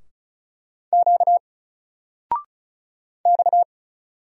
Morse code, heard as a keyed tone, about 36 wpm, 700 Hz. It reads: Q X
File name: Single-Letter-Number-35wpm